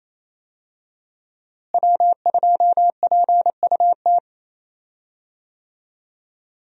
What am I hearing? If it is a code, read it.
Morse code, heard as a keyed tone, about 28 wpm, 700 Hz. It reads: W2PUT
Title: USA Call Signs - 5 Characters - Repeated 2x - 28wpm